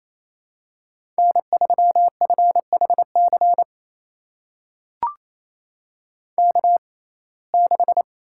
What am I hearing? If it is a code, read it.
Morse code, heard as a keyed tone, about 28 wpm, spaced as for 20 wpm, 700 Hz. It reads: N3FHC K6
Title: USA Call Signs - 5 Characters - 6x Character Spacing - 28wpm